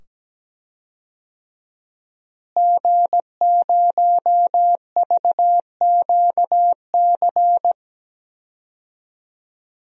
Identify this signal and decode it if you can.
Morse code, heard as a keyed tone, about 17 wpm, 700 Hz. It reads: G0VQC
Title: DXCC-100 Call Signs - 5 Characters - Repeated 2x - 17wpm